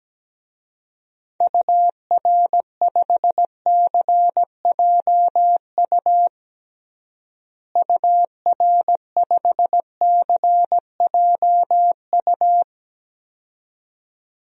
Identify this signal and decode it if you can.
Morse code, heard as a keyed tone, about 17 wpm, 700 Hz. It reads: UR5CJU UR5CJU
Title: DXCC-100 Call Signs - 6 Characters - Repeated 2x - 17wpm